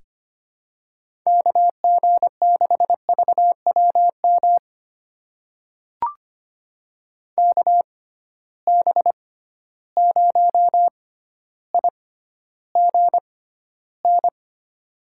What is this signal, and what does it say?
Morse code, heard as a keyed tone, about 25 wpm, spaced as for 13 wpm, 700 Hz. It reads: KG6VWM KB0IGN
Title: USA Call Signs - 6 Characters - 6x Character Spacing - 25wpm